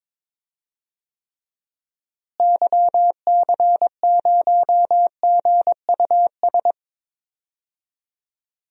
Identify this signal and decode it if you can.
Morse code, heard as a keyed tone, about 22 wpm, 700 Hz. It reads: YC0GUS
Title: DXCC-100 Call Signs - 6 Characters - Repeated 3x - 22wpm